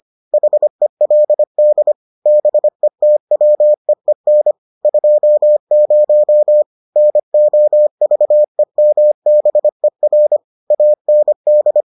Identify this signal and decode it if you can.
Morse code, heard as a keyed tone, about 25 wpm, 600 Hz. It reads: HELD BETWEEN 20 NOVEMBER AND